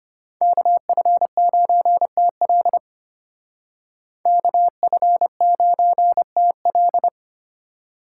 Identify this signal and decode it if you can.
Morse code, heard as a keyed tone, about 32 wpm, 700 Hz. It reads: KF9TL KF9TL